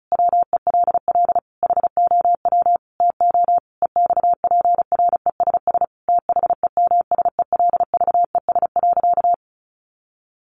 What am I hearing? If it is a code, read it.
Morse code, heard as a keyed tone, about 34 wpm, 700 Hz. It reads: WELL HOW TO EXPRESS THEMSELVES.
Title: Sentences from Top 1100 Words 35wpm